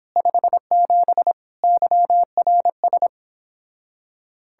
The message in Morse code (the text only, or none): 57 YRS